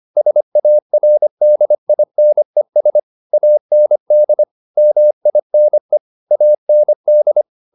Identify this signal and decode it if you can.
Morse code, heard as a keyed tone, about 25 wpm, 600 Hz. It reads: SARDINES AND MINE AND
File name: the old man and the sea part 1 25wpm